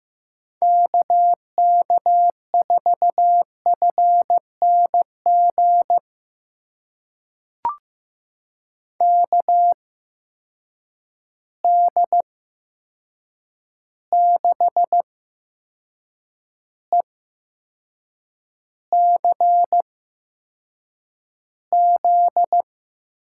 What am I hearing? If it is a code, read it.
Morse code, heard as a keyed tone, about 15 wpm, spaced as for 6 wpm, 700 Hz. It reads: KK4FNG KD6ECZ
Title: USA Call Signs - 6 Characters - 8x Character Spacing - 15wpm